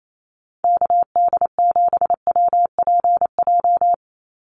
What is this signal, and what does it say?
Morse code, heard as a keyed tone, about 28 wpm, 700 Hz. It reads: KD7WPJ
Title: NA SOTA Activator Callsigns 28wpm